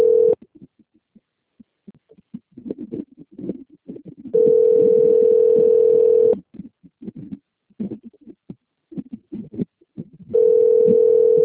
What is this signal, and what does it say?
Ringback tone (North America), 440 + 480 Hz.
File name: out- - - - - .2962